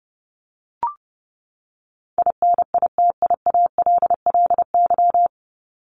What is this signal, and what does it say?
Morse code, heard as a keyed tone, about 30 wpm, 700 Hz. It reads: INITIALLY
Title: Top 1251-1500 Words 30wpm